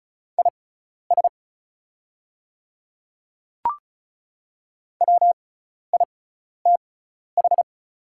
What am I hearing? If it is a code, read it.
Morse code, heard as a keyed tone, about 36 wpm, spaced as for 12 wpm, 700 Hz. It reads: IS WITH
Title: 6x Character Spacing - Top 100 Words - 35wpm